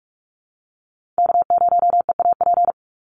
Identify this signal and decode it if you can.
Morse code, heard as a keyed tone, about 44 wpm, 700 Hz. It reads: K0EAP
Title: NA SOTA Chaser Callsigns 45wpm